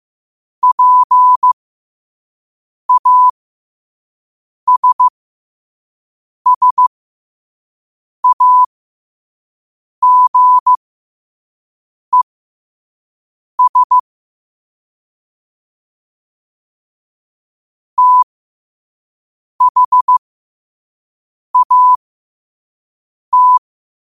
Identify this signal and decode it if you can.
Morse code, heard as a keyed tone, about 15 wpm, spaced as for 5 wpm, 1000 Hz. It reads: PASSAGES THAT
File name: qotdfiles / 5wpm